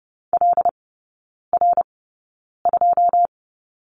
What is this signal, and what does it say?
Morse code, heard as a keyed tone, about 30 wpm, spaced as for 9 wpm, 700 Hz. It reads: LR2
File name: course / 30:10wpm / 087 Lesson 15e - Number 5 - Three Characters 30wpm